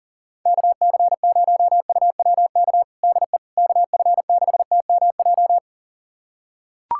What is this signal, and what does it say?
Morse code, heard as a keyed tone, about 37 wpm, 700 Hz. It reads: KC0UWK DE KF6TMJ
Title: USA Callsign DE Callsign - 6 Characters - 40wpm